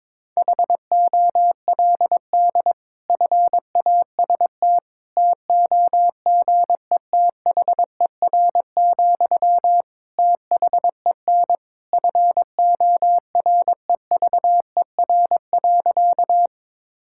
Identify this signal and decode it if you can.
Morse code, heard as a keyed tone, about 22 wpm, 700 Hz. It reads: HOLD FAST TOGETHER, THEN FOREVER.
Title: Sentences from Top 1600 Words 22wpm